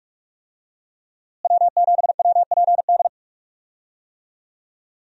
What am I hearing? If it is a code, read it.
Morse code, heard as a keyed tone, about 46 wpm, 700 Hz. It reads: W7WPD
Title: USA Call Signs - 5 Characters - Repeated 2x - 45wpm